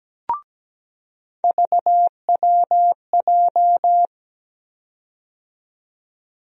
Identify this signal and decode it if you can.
Morse code, heard as a keyed tone, about 17 wpm, 700 Hz. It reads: VWJ
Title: USA Call Sign Postfixes - 3 Letters 17wpm